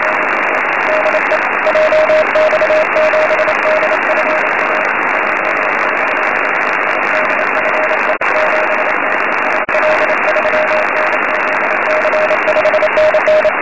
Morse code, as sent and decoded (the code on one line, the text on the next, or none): none